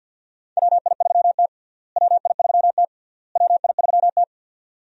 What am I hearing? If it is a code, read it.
Morse code, heard as a keyed tone, about 50 wpm, 700 Hz. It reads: WI3T WI3T WI3T
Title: USA Call Signs - 4 Characters - Repeated 3x - 50wpm